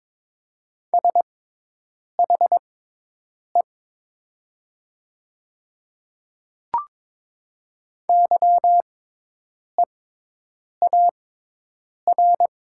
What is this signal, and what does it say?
Morse code, heard as a keyed tone, about 22 wpm, spaced as for 8 wpm, 700 Hz. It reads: SHE YEAR
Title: 6x Character Spacing - Top 200 Words - 22wpm